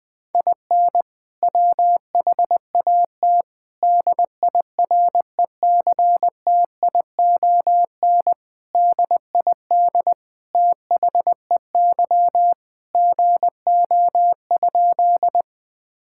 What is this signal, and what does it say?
Morse code, heard as a keyed tone, about 20 wpm, 700 Hz. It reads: IN WHAT DIRECTION DID THEY GO?
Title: Sentences from Top 1000 Words - Review Part 5 - 20wpm